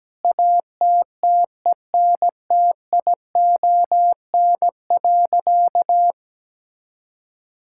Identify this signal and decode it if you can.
Morse code, heard as a keyed tone, about 17 wpm, 700 Hz. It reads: ATTENTION.
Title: Sentences from Top 1000 Words - Review Part 6 - 17wpm